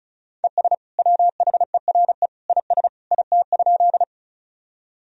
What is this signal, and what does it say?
Morse code, heard as a keyed tone, about 36 wpm, 700 Hz. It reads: ES WHERE IS IT?
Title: Sentences from Top 200 Words and Abbreviations 35wpm